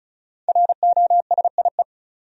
Morse code, heard as a keyed tone, about 34 wpm, 700 Hz. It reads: ROSIE
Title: USA Common Names - 35wpm